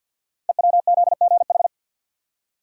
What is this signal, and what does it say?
Morse code, heard as a keyed tone, about 49 wpm, 700 Hz. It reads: EWZGH
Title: Code Groups - 5 Letters - 50wpm